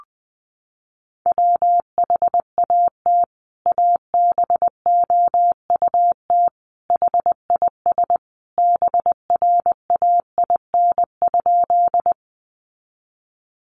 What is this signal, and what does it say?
Morse code, heard as a keyed tone, about 20 wpm, 700 Hz. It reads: WHAT ABOUT HIS BRAIN?
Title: Sentences from Top 1100 Words 20wpm